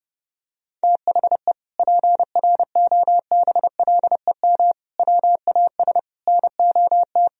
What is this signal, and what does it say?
Morse code, heard as a keyed tone, about 30 wpm, 700 Hz. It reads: THE PROBLEM WAS NOT
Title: Sentences from Top 1000 Words - Review Part 2 - 30wpm